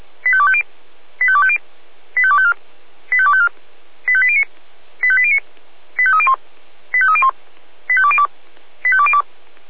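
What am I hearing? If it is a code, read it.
Five-tone selective call (ZVEI), heard as quick runs of five tones, tones of about 70 ms. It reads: 85250 85250 85254 85254 85809 85809 85301 85301 85202 85202
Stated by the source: ZVEI five-tone call received by SDR, Feuerwehr Orlamünde.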